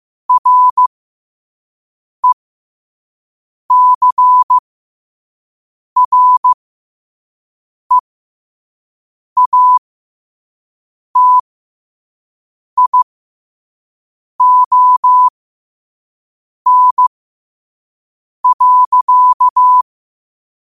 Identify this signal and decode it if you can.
Morse code, heard as a keyed tone, about 15 wpm, spaced as for 5 wpm, 1000 Hz. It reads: RECREATION.
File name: qotdfiles / 5wpm